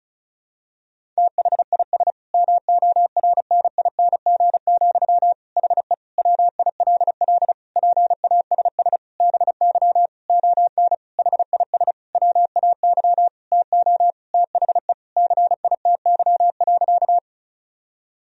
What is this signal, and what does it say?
Morse code, heard as a keyed tone, about 34 wpm, 700 Hz. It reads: THIS MORNING, HE WILL PASS BY ON HIS WAY TO THE CITY.
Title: Sentences from Top 1000 Words - Review Part 1 - 35wpm